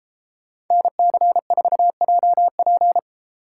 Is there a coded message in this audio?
Yes: Morse code from 0.7 to 3.0 s, heard as a keyed tone, about 33 wpm, 700 Hz. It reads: NC4JP